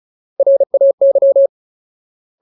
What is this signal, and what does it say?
Morse code, heard as a keyed tone, about 34 wpm, 550 Hz. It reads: RAY
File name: Ultimate Word List / 35WPM Ultimate